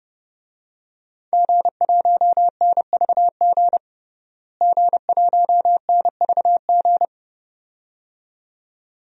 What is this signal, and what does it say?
Morse code, heard as a keyed tone, about 30 wpm, 700 Hz. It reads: G1NVG G1NVG
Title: DXCC-100 Call Signs - 5 Characters - Repeated 2x - 30wpm